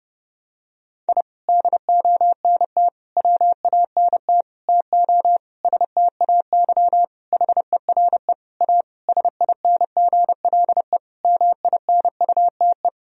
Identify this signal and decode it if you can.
Morse code, heard as a keyed tone, about 30 wpm, 700 Hz. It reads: I DONT WANT TO STAY HERE A SINGLE MINUTE
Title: Sentences from Top 1000 Words - Review Part 4 - 30wpm